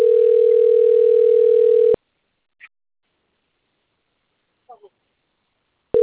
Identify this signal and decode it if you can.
Ringback tone (North America), 440 + 480 Hz.